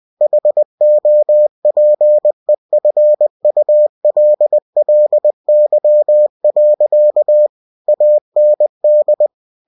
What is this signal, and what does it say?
Morse code, heard as a keyed tone, about 20 wpm, 600 Hz. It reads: HOPEFULLY. AND
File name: cw-books / winnie-the-pooh / winnie-20wpm-page-0025